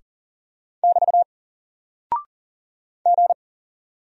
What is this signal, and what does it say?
Morse code, heard as a keyed tone, about 39 wpm, 700 Hz. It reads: <BT> G
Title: Single Character (Letter:Number:Punctuation:Prosign) 40wpm